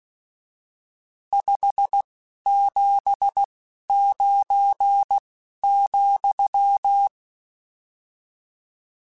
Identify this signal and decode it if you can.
Morse code, heard as a keyed tone, about 16 wpm, 800 Hz. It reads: 5 7 9 ,
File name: RSTs / RST-mp3s / Various RST @10